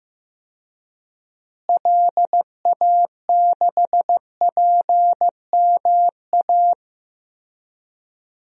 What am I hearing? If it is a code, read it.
Morse code, heard as a keyed tone, about 15 wpm, 700 Hz. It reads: LA6PMA
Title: DXCC-100 Call Signs - 6 Characters - Repeated 2x - 15wpm